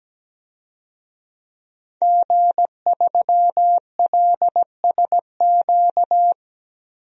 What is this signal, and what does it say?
Morse code, heard as a keyed tone, about 17 wpm, 700 Hz. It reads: G3LSQ
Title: DXCC-100 Call Signs - 5 Characters - 17wpm